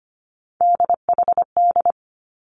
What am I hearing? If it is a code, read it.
Morse code, heard as a keyed tone, about 25 wpm, 700 Hz. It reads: DHD
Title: Three Challenging Letter-Number Combinations 25wpm